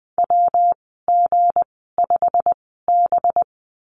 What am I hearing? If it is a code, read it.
Morse code, heard as a keyed tone, about 20 wpm, spaced as for 15 wpm, 700 Hz. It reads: WG5B
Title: 200 Lesson 32g - Letter k - Call Signs 20wpm